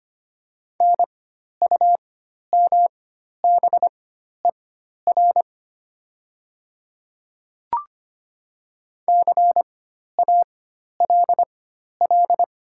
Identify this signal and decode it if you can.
Morse code, heard as a keyed tone, about 25 wpm, spaced as for 12 wpm, 700 Hz. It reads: NUMBER CALL
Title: 4x Character Spacing - Top 100 Words - 25wpm